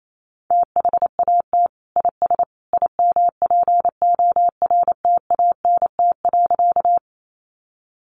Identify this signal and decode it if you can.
Morse code, heard as a keyed tone, about 28 wpm, 700 Hz. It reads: THAT IS IMPORTANT.
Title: Sentences from Top 600 Words 28wpm